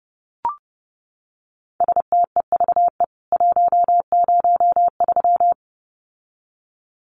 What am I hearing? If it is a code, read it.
Morse code, heard as a keyed tone, about 30 wpm, 700 Hz. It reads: STEVE 103
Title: CWT Exchange 30wpm